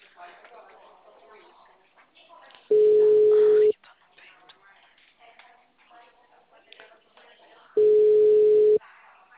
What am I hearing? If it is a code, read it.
Ringback tone (Europe, 425 Hz), with speech.
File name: out- -2007- - - .87818